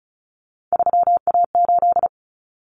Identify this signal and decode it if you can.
Morse code, heard as a keyed tone, about 34 wpm, 700 Hz. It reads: 3A8
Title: Common Cut Numbers - 3-Digit 35wpm